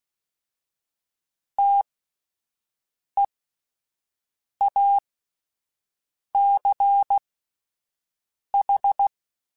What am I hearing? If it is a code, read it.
Morse code, heard as a keyed tone, about 16 wpm, spaced as for 5 wpm, 800 Hz. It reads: TEACH
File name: (spaced) / QSOs Spaced-mp3s / WD0BGP 1 @10 sp